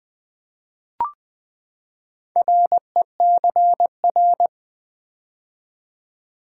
Morse code, heard as a keyed tone, about 20 wpm, 700 Hz. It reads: RECR
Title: Build A Word - 4 Letters At A Time - 11-Letter Words - 20wpm